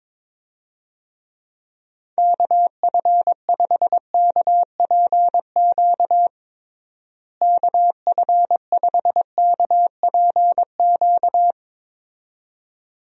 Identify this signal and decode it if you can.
Morse code, heard as a keyed tone, about 22 wpm, 700 Hz. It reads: KF5KPQ KF5KPQ
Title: USA Call Signs - 6 Characters - Repeated 2x - 22wpm